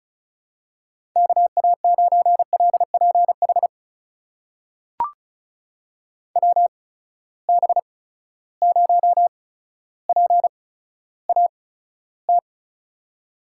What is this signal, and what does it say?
Morse code, heard as a keyed tone, about 36 wpm, spaced as for 15 wpm, 700 Hz. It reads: KA9LPH WB0PAT
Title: USA Call Signs - 6 Characters - 8x Character Spacing - 35wpm